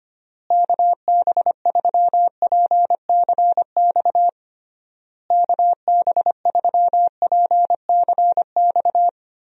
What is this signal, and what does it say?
Morse code, heard as a keyed tone, about 25 wpm, 700 Hz. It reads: KB3PCX KB3PCX
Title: USA Call Signs - 6 Characters - Repeated 3x - 25wpm